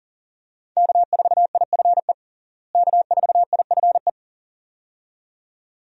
Morse code, heard as a keyed tone, about 40 wpm, 700 Hz. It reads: K4IFE K4IFE
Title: USA Call Signs - 5 Characters - Repeated 2x - 40wpm